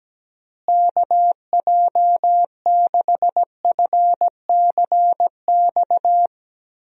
Morse code, heard as a keyed tone, about 17 wpm, 700 Hz. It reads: KJ6FCX
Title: USA Call Signs - 6 Characters - Repeated 3x - 17wpm